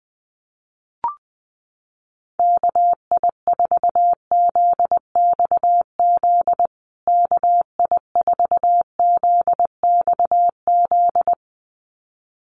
Morse code, heard as a keyed tone, about 20 wpm, 700 Hz. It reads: KI4ZXZ KI4ZXZ
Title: USA Call Signs Repeated 2x 20wpm